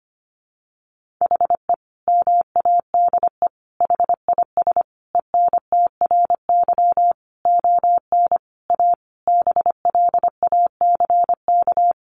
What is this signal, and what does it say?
Morse code, heard as a keyed tone, about 25 wpm, 700 Hz. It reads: HE MADE HIS ENTRY ON A BLACK
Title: Sentences from Top 1500 Words 25wpm